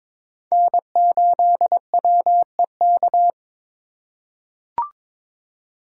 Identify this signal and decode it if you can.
Morse code, heard as a keyed tone, about 23 wpm, 700 Hz. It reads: N8WEK
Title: USA Call Signs - 5 Characters - 2x Character Spacing - 22wpm